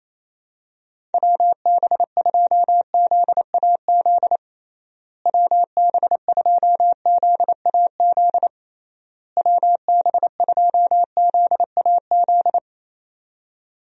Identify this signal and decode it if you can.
Morse code, heard as a keyed tone, about 28 wpm, 700 Hz. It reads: WB2ZAZ WB2ZAZ WB2ZAZ
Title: USA Call Signs - 6 Characters - Repeated 3x - 28wpm